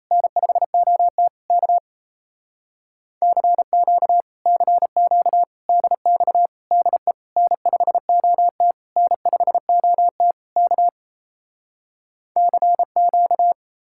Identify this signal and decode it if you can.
Morse code, heard as a keyed tone, about 38 wpm, 700 Hz. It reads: N5OT K CQ CQ DX DE N5OT N5OT K CQ